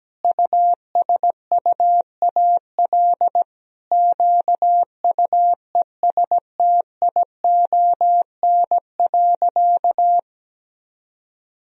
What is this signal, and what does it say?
Morse code, heard as a keyed tone, about 17 wpm, 700 Hz. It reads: USUAL QUESTION.
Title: Sentences from Top 1000 Words - Review Part 2 - 17wpm